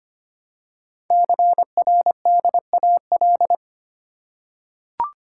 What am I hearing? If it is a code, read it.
Morse code, heard as a keyed tone, about 25 wpm, 700 Hz. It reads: CRDAL
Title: Code Groups - 5 Letters - 25wpm